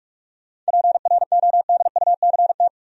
Morse code, heard as a keyed tone, about 58 wpm, 700 Hz. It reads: PRODUCT